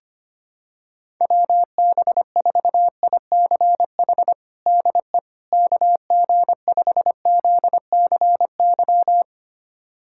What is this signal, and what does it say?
Morse code, heard as a keyed tone, about 25 wpm, 700 Hz. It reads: WB4ICH DE KG5ZCY
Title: USA Callsign DE Callsign - 6 Characters - 25wpm